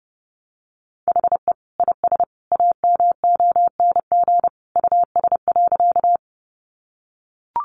Morse code, heard as a keyed tone, about 30 wpm, 700 Hz. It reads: HE IS AMONG US.
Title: Sentences from Top 500 Words 30wpm